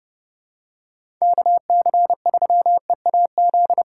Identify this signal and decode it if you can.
Morse code, heard as a keyed tone, about 30 wpm, 700 Hz. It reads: KC3EAZ
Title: USA Call Signs - 6 Characters - 6x Character Spacing - 30wpm